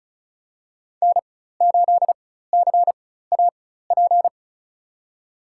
Morse code, heard as a keyed tone, about 36 wpm, spaced as for 16 wpm, 700 Hz. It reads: N8CAP
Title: USA Call Signs - 4x Character Spacing - 35wpm